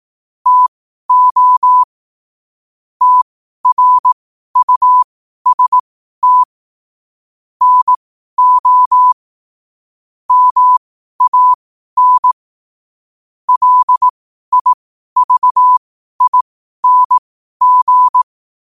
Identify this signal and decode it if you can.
Morse code, heard as a keyed tone, about 18 wpm, spaced as for 12 wpm, 1000 Hz. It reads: TO TRUST NO MAN LIVING
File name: qotdfiles / 12wpm